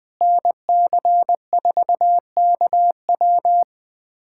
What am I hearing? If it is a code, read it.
Morse code, heard as a keyed tone, about 20 wpm, 700 Hz. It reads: NC4KW
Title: USA Callsign DE Callsign - 5 Characters - 20wpm